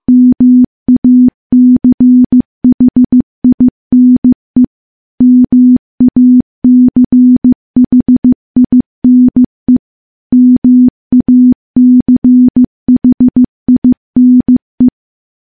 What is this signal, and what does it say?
Morse code, heard as a keyed tone, about 15 wpm, 250 Hz. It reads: MACHINE MACHINE MACHINE
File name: morse / morse-training-15WPM